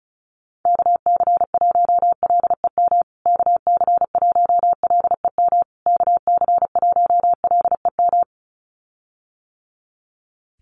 Morse code, heard as a keyed tone, about 34 wpm, 700 Hz. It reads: KC1LEM KC1LEM KC1LEM
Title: USA Call Signs Repeated 3x 35wpm